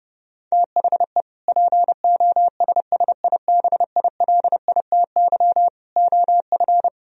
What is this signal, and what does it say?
Morse code, heard as a keyed tone, about 33 wpm, 700 Hz. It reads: THE POSSIBILITY OF